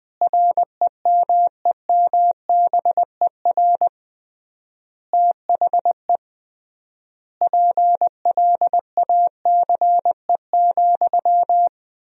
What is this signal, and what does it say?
Morse code, heard as a keyed tone, about 20 wpm, 700 Hz. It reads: REMEMBER THE PLACE,
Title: 3x Word-Spacing - Sentences from Top 400 Words - 20wpm